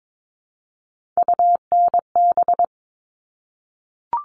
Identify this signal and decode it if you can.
Morse code, heard as a keyed tone, about 22 wpm, 700 Hz. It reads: UNB
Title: All Cut Numbers - 3-Digit 22wpm